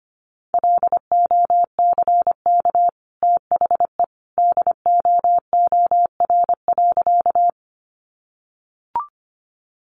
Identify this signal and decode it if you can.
Morse code, heard as a keyed tone, about 25 wpm, 700 Hz. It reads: LOCK THE DOOR.
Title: Sentences from Top 1300 Words 25wpm